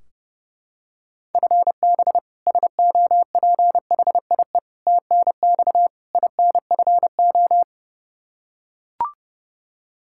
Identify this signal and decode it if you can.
Morse code, heard as a keyed tone, about 32 wpm, 700 Hz. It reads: FB SOPHIE TNX INFO
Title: K1USN SST QSO Practice 30wpm